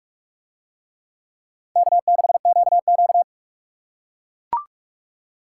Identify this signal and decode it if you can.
Morse code, heard as a keyed tone, about 43 wpm, 700 Hz. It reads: K6QQ